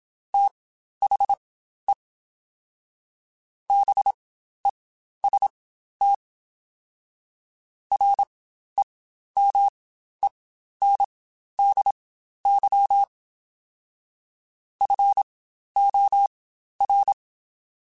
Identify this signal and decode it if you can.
Morse code, heard as a keyed tone, about 27 wpm, spaced as for 12 wpm, 800 Hz. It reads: THE BEST REMENDY FOR
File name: Quotes Spaced / Quotes Spaced-mp3s / Temper sp @25